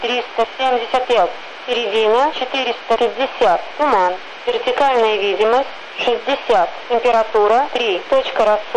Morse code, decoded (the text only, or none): none